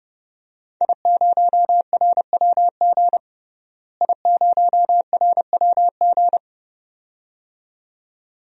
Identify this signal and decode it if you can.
Morse code, heard as a keyed tone, about 30 wpm, 700 Hz. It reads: I0RWG I0RWG
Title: DXCC-100 Call Signs - 5 Characters - Repeated 2x - 30wpm